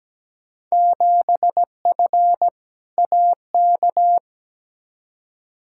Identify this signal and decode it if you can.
Morse code, heard as a keyed tone, about 17 wpm, 700 Hz. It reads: 7F AK
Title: Field Day Exchanges Encore - 17wpm